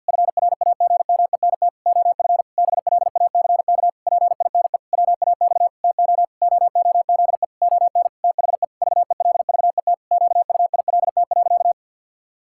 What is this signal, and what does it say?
Morse code, heard as a keyed tone, about 51 wpm, 700 Hz. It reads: FRAGMENT OF BLACK PINE WAX TO OOZE ON THE VELVET QUILT.
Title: Pangram Challenge v2 - 50wpm